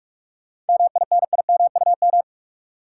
Morse code, heard as a keyed tone, about 46 wpm, 700 Hz. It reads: MINIMUM
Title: Seven Letter Words - 45wpm